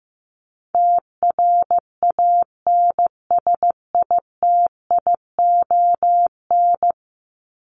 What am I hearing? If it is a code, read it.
Morse code, heard as a keyed tone, about 15 wpm, 700 Hz. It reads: TRANSITION
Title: Top 1751-2000 Words 15wpm